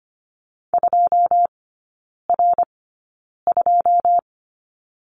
Morse code, heard as a keyed tone, about 25 wpm, spaced as for 9 wpm, 690 Hz. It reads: 2R2
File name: Lesson 14e - Number 2 - Three Characters 25wpm